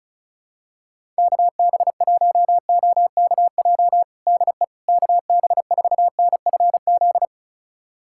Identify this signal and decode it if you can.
Morse code, heard as a keyed tone, about 36 wpm, 700 Hz. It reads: KB1OKJ DE KB4NFZ
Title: USA Callsign DE Callsign 35wpm